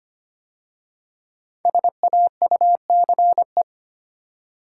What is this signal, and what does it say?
Morse code, heard as a keyed tone, about 25 wpm, 700 Hz. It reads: SAUCE
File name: 5 Letter Words - First 3 Letters Given - 25wpm